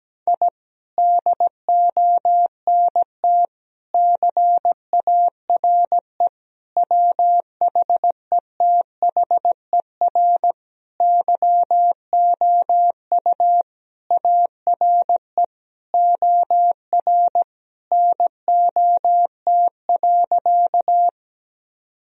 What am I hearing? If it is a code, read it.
Morse code, heard as a keyed tone, about 17 wpm, 700 Hz. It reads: I DONT CARE WHETHER YOU ARE OR NOT.
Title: Sentences from Top 1000 Words - Review Part 3 - 17wpm